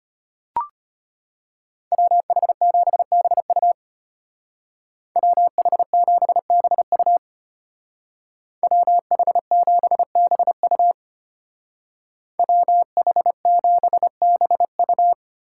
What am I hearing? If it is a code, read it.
Morse code, heard as a keyed tone, about 33 wpm, 700 Hz. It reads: WH7BU WH7BU WH7BU WH7BU